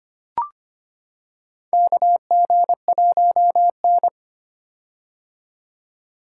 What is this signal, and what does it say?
Morse code, heard as a keyed tone, about 25 wpm, 700 Hz. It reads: KG1N
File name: practice / 25wpm / Four Letter-Number Combinations - 25wpm